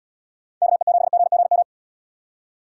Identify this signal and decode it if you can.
Morse code, heard as a keyed tone, about 77 wpm, 700 Hz. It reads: KE7GGM